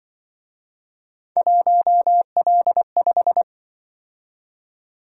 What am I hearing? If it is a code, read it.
Morse code, heard as a keyed tone, about 25 wpm, 700 Hz. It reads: 1L5
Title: Speed-Racing - Three Letter-Number Combinations - 20wpm